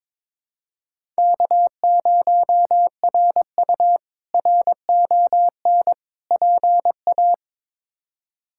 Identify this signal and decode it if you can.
Morse code, heard as a keyed tone, about 22 wpm, 700 Hz. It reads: K0RU RON PA
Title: K1USN SST QSO Practice - 22wpm